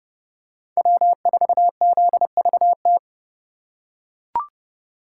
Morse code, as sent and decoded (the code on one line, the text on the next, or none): .-- ....- --.. ...- -
W4ZVT